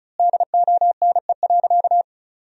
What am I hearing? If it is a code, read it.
Morse code, heard as a keyed tone, about 36 wpm, 700 Hz. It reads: DONE.
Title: 3x Word-Spacing - Sentences from Top 500 Words - 35wpm